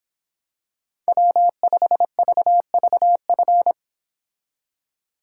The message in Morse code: W5VVF

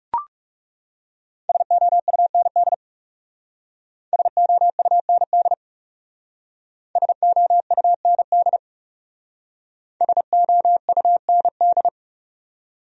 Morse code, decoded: SOUND SOUND SOUND SOUND

